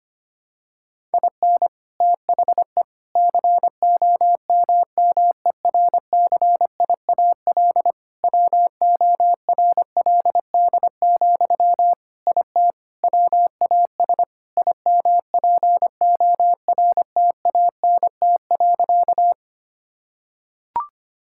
Morse code, heard as a keyed tone, about 25 wpm, 700 Hz. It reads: IN THE COMMERCIAL WORLD, IT WAS IMPORTANT.
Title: Sentences from Top 1000 Words - Review Part 5 - 25wpm